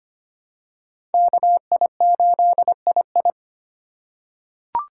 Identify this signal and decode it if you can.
Morse code, heard as a keyed tone, about 25 wpm, 700 Hz. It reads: KI8II
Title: USA Call Signs - 4x Character Spacing - 25wpm